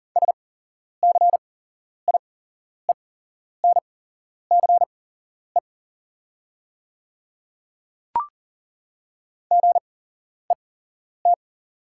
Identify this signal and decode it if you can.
Morse code, heard as a keyed tone, about 40 wpm, spaced as for 11 wpm, 700 Hz. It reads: SCIENCE GET
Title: 8x Character Spacing - Top 500 Words - 40wpm